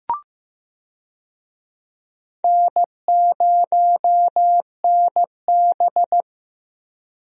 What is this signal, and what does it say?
Morse code, heard as a keyed tone, about 16 wpm, 700 Hz. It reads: N0NB
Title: USA Call Signs - 4 Characters - Repeated 3x - 15wpm